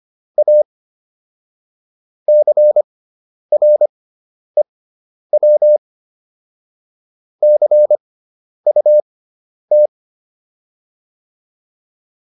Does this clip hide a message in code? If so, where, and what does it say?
Yes: Morse code from 0.4 to 9.9 s, heard as a keyed tone, about 25 wpm, spaced as for 10 wpm, 600 Hz. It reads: A CREW CUT